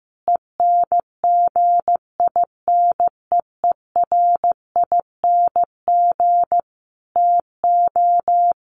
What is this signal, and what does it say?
Morse code, heard as a keyed tone, about 14 wpm, 700 Hz. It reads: ENGINEERING TO
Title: Sentences from Top 1500 Words 15wpm